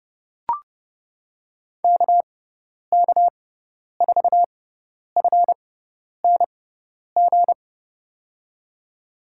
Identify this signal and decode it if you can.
Morse code, heard as a keyed tone, about 31 wpm, spaced as for 10 wpm, 700 Hz. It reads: KK4FNG
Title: USA Call Signs - 6x Character Spacing - 30wpm